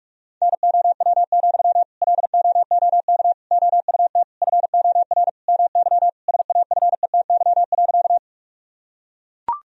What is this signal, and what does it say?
Morse code, heard as a keyed tone, about 42 wpm, 700 Hz. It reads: NOW, LOOK OUT FOR MY SAFETY.
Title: Sentences from Top 1000 Words - Review Part 4 - 45wpm